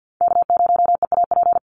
Morse code, heard as a keyed tone, about 50 wpm, 700 Hz. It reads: K0EAP
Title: NA SOTA Chaser Callsigns 50wpm